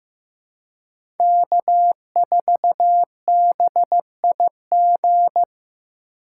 Morse code, heard as a keyed tone, about 15 wpm, 700 Hz. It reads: K4BIG
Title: USA Call Signs - 5 Characters - 4x Character Spacing - 15wpm